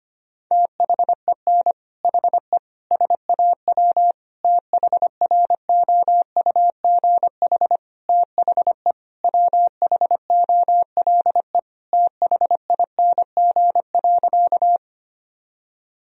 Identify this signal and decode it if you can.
Morse code, heard as a keyed tone, about 25 wpm, 700 Hz. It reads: THEN HE SAW THROUGH THE WHOLE THING.
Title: Sentences from Top 1000 Words - Review Part 1 - 25wpm